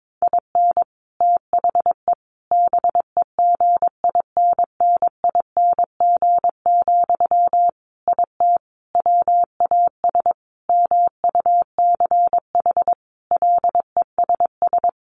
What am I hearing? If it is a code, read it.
Morse code, heard as a keyed tone, about 22 wpm, 700 Hz. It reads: IN THE BEGINNING, IT WAS MUCH LESS
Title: Sentences from Top 1400 Words 22wpm